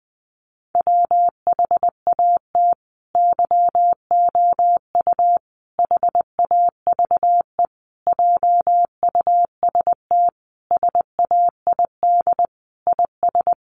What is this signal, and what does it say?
Morse code, heard as a keyed tone, about 20 wpm, 700 Hz. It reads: WHAT YOU HAVE JUST SAID IS
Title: Sentences from Top 200 Words 20wpm